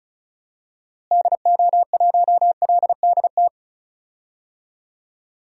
Morse code, heard as a keyed tone, about 34 wpm, 700 Hz. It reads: DO1LDT
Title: DXCC-100 Call Signs - 6 Characters - Repeated 2x - 35wpm